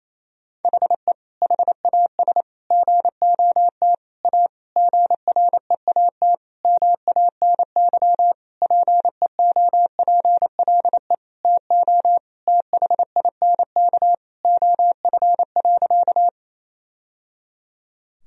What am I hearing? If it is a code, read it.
Morse code, heard as a keyed tone, about 27 wpm, 700 Hz. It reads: HE HAS GOT A GREAT MANY PEOPLE TO THINK OF.
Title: Sentences from Top 1000 Words - Review Part 1 - 28wpm